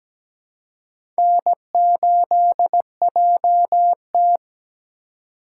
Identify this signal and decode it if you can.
Morse code, heard as a keyed tone, about 17 wpm, 700 Hz. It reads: N8JT
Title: USA Call Signs - 3x Character Spacing - 17wpm